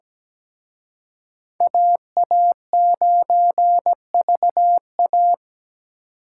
Morse code, heard as a keyed tone, about 20 wpm, 700 Hz. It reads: AA9VA
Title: Speed Racing - USA 2x2 Call Signs 17wpm